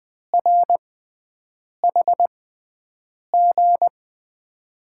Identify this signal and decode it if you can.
Morse code, heard as a keyed tone, about 21 wpm, spaced as for 7 wpm, 700 Hz. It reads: RHG
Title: USA Call Signs - 5 Characters - 6x Character Spacing - 20wpm